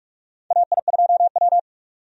Morse code, heard as a keyed tone, about 46 wpm, 700 Hz. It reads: AI2W
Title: USA Call Signs - 4 Characters - 45wpm